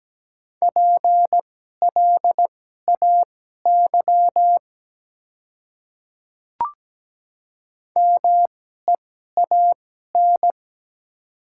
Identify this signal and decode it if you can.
Morse code, heard as a keyed tone, about 17 wpm, spaced as for 12 wpm, 690 Hz. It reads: PLAY MEAN
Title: 2x Character Spacing - Top 300 Words - 17wpm